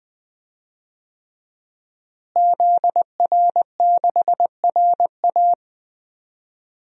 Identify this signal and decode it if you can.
Morse code, heard as a keyed tone, about 20 wpm, 700 Hz. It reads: ZR6RA
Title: DXCC-100 Call Signs - 5 Characters - Repeated 3x - 20wpm